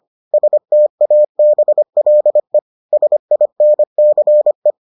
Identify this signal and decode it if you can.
Morse code, heard as a keyed tone, about 25 wpm, 600 Hz. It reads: STABLE SINCE